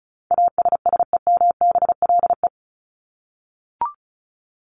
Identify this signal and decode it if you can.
Morse code, heard as a keyed tone, about 34 wpm, 700 Hz. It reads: ASSEMBLE
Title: LY Suffix - Top 100 Words with Base Words 35wpm